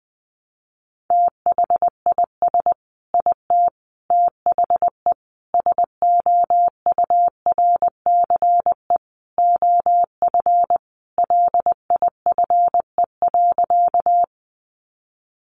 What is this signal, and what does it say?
Morse code, heard as a keyed tone, about 20 wpm, 700 Hz. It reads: THIS IT THE SOURCE OF LIFE.
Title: Sentences from Top 800 Words 20wpm